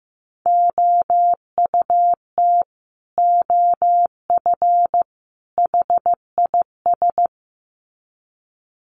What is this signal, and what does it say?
Morse code, heard as a keyed tone, about 15 wpm, 700 Hz. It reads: OUT OF HIS
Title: Sets of 3 Words 15wpm